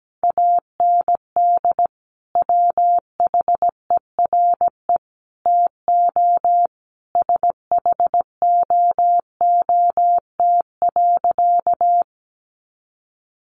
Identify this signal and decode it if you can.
Morse code, heard as a keyed tone, about 17 wpm, 700 Hz. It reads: AND WHERE TO SHOOT.
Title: A Sound Of Thunder 17wpm